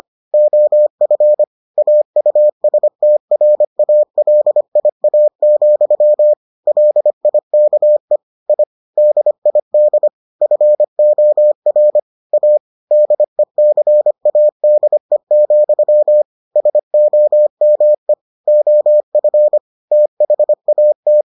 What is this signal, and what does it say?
Morse code, heard as a keyed tone, about 25 wpm, 600 Hz. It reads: OF AUSTRALIA, LIKE I DID FOR A DECADE, SOME OF THAT